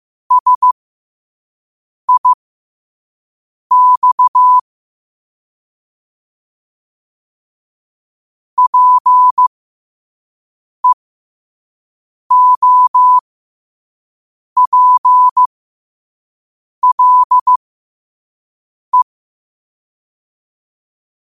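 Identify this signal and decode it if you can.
Morse code, heard as a keyed tone, about 15 wpm, spaced as for 5 wpm, 1000 Hz. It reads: SIX PEOPLE
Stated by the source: Morse code quote of the day sent at 5 wpm.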